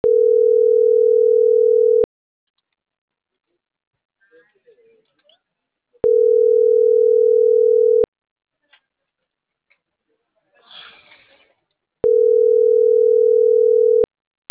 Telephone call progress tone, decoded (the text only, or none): ringback tone